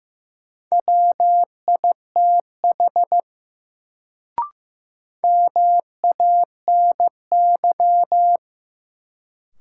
Morse code, heard as a keyed tone, about 15 wpm, 700 Hz. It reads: WITH MANY
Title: Sentences from Top 100 Words - New Word Drill - 15wpm